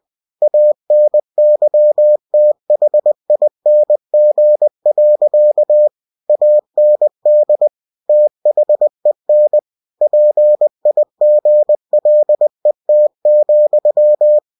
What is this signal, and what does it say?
Morse code, heard as a keyed tone, about 20 wpm, 600 Hz. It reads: ANYTHING. AND THEN PIGLET,
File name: cw-books / winnie-the-pooh / winnie-20wpm-page-0064